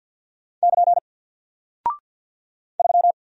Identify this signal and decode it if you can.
Morse code, heard as a keyed tone, about 48 wpm, 700 Hz. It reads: <KN> 3
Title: Single Character (Letter:Number:Punctuation:Prosign) Alt 50wpm